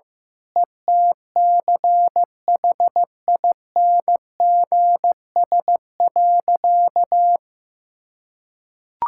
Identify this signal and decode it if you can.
Morse code, heard as a keyed tone, about 15 wpm, 700 Hz. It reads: ETCHINGS.
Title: Pangram Challenge v2 - 15wpm